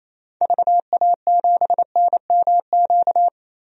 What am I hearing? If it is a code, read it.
Morse code, heard as a keyed tone, about 27 wpm, 700 Hz. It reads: VA7NMQ
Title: DXCC-100 Call Signs - 6 Characters - 28wpm